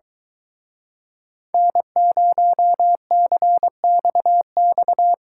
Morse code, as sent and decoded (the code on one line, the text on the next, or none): -. ----- -.-. -..- -..-
N0CXX